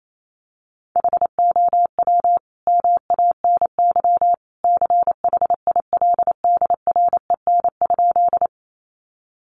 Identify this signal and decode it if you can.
Morse code, heard as a keyed tone, about 28 wpm, 700 Hz. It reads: HOW MANY CHILDREN?
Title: Sentences from Top 300 Words 28wpm